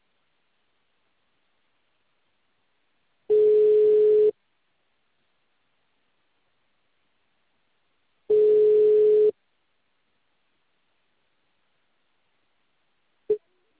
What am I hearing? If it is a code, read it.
Ringback tone (Europe, 425 Hz).